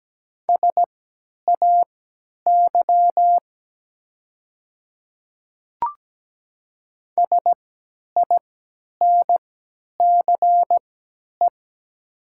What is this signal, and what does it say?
Morse code, heard as a keyed tone, about 17 wpm, spaced as for 10 wpm, 700 Hz. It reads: SAY SINCE
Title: 3x Character Spacing - Top 300 Words - 17wpm